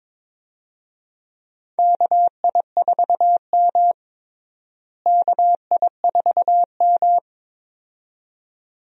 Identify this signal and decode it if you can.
Morse code, heard as a keyed tone, about 21 wpm, 700 Hz. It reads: KI4M KI4M
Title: USA Call Signs - 4 Characters - Repeated 2x - 22wpm